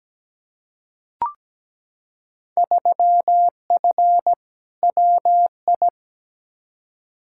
Morse code, heard as a keyed tone, about 17 wpm, 700 Hz. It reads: 3F WI
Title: Field Day Exchanges - 17wpm